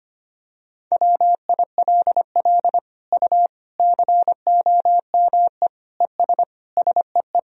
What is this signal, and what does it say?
Morse code, heard as a keyed tone, about 25 wpm, 700 Hz. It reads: WILL U COME ES SEE
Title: Sentences from Top 100 Words and Abbreviations 25wpm